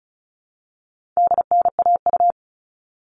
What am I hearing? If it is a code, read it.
Morse code, heard as a keyed tone, about 34 wpm, 700 Hz. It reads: DNAU
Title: All Cut Numbers - 4-Digit 35wpm